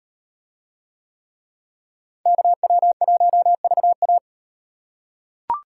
Morse code, heard as a keyed tone, about 38 wpm, 700 Hz. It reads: KW1VA